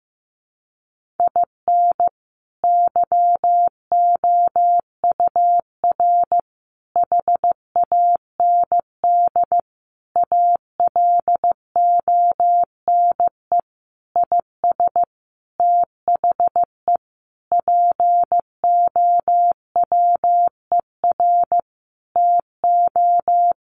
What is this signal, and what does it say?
Morse code, heard as a keyed tone, about 14 wpm, 700 Hz. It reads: IN YOUR HAND ALONE IS THE POWER TO
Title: Sentences from Top 1400 Words 15wpm